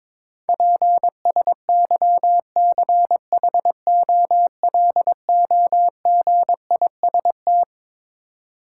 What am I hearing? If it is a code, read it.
Morse code, heard as a keyed tone, about 22 wpm, 700 Hz. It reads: PSYCHOLOGIST
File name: Twelve Letter Words - 22wpm